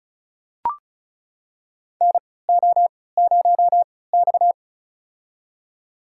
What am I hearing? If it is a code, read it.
Morse code, heard as a keyed tone, about 36 wpm, spaced as for 20 wpm, 700 Hz. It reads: NO0X
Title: USA Call Signs - 4 Characters - 3x Character Spacing - 35wpm